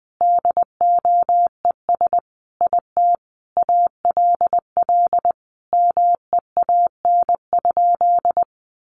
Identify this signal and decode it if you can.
Morse code, heard as a keyed tone, about 19 wpm, 700 Hz. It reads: DOES IT ALL MEAN?
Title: Sentences from Top 200 Words 20wpm